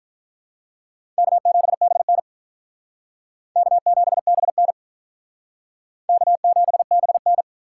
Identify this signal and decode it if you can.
Morse code, heard as a keyed tone, about 46 wpm, 700 Hz. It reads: K7BN K7BN K7BN